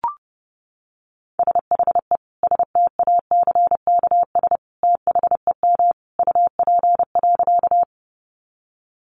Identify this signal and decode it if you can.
Morse code, heard as a keyed tone, about 30 wpm, 700 Hz. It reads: SHE STACKS THEM UP.
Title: Easy Phrases 30wpm